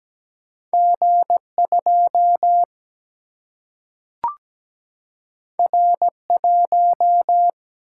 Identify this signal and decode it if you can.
Morse code, heard as a keyed tone, about 17 wpm, 700 Hz. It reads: G2 R1
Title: Single-Letter Single-Number 17wpm